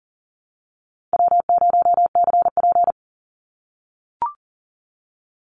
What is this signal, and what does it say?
Morse code, heard as a keyed tone, about 40 wpm, 700 Hz. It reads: W0CP
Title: NA SOTA Chaser Callsigns 40wpm